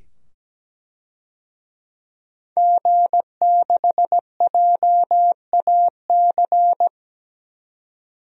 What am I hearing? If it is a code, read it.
Morse code, heard as a keyed tone, about 17 wpm, 700 Hz. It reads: G6JAC
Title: DXCC-100 Call Signs - 5 Characters - 17wpm